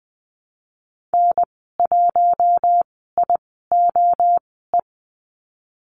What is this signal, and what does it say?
Morse code, heard as a keyed tone, about 20 wpm, spaced as for 15 wpm, 700 Hz. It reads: N1IOE
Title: 088 Lesson 15g - Number 5 - Call Signs 20wpm